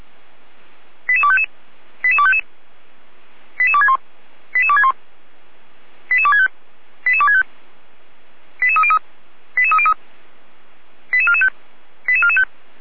Five-tone selective call (ZVEI), heard as quick runs of five tones, tones of about 70 ms. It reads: 88255 88255 88261 88261 88265 88265 88303 88303 88445 88445